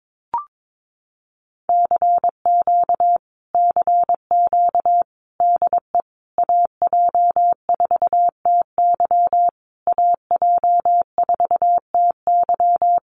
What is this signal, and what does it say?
Morse code, heard as a keyed tone, about 22 wpm, 700 Hz. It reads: CQ CQ DE AJ4TY AJ4TY
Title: Common QSO Phrases Encore 22wpm